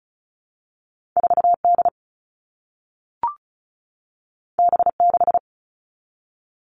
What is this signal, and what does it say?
Morse code, heard as a keyed tone, about 36 wpm, 700 Hz. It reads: 4D B6
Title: All Cut Numbers - 2-Digit 35wpm